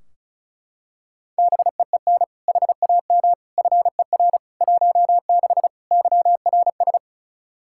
Morse code, heard as a keyed tone, about 34 wpm, 700 Hz. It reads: BEEN HAM FER 16 YRS
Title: Practice QSO - Exchange 3 - More Info - 35wpm